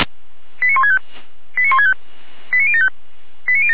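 Five-tone selective call (ZVEI), heard as quick runs of five tones, tones of about 70 ms. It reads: 89165 89165 89075
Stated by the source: ZVEI five-tone selective call received by SDR.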